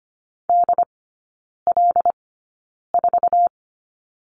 Morse code, heard as a keyed tone, about 25 wpm, spaced as for 9 wpm, 700 Hz. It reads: DL4
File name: Lesson 15e - Number 5 - Three Characters 25wpm